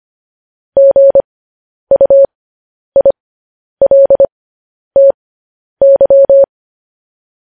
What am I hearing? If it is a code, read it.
Morse code, heard as a keyed tone, about 25 wpm, spaced as for 10 wpm, 550 Hz. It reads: GUILTY
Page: news headlines in morse code at 10 wpm